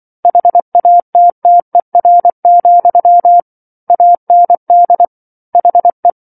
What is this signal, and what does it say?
Morse code, heard as a keyed tone, about 24 wpm, 700 Hz. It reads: HATTER, AND HE